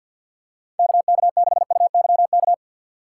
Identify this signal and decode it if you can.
Morse code, heard as a keyed tone, about 48 wpm, 700 Hz. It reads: KK6UYK
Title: USA Call Signs - 6x Character Spacing - 50wpm